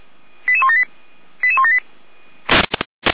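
Five-tone selective call (ZVEI), heard as quick runs of five tones, tones of about 70 ms. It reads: 88178 88178
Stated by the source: ZVEI five-tone call 88179 received by SDR.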